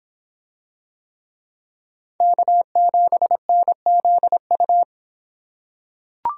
Morse code, heard as a keyed tone, about 26 wpm, 700 Hz. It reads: K7NZU